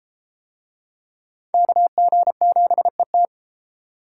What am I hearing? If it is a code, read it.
Morse code, heard as a keyed tone, about 30 wpm, 700 Hz. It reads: KG7ET